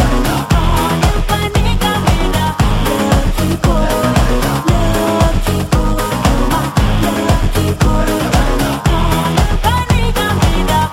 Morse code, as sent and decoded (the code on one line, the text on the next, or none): none